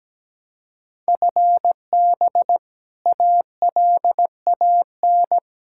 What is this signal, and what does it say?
Morse code, heard as a keyed tone, about 17 wpm, 700 Hz. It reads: FB ALAN
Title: Practice QSO - Exchange 3 - More Info - 17wpm